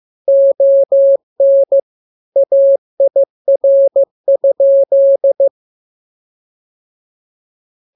Morse code, heard as a keyed tone, about 15 wpm, 550 Hz. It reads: ON AIR?